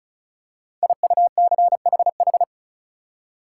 Morse code, heard as a keyed tone, about 36 wpm, 700 Hz. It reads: IUCHH
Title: Code Groups - 5 Letters - 35wpm